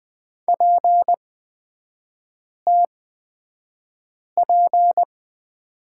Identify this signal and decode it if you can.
Morse code, heard as a keyed tone, about 20 wpm, spaced as for 5 wpm, 700 Hz. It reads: PTP
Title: 250 Lesson 40e - BK - Three Characters 20wpm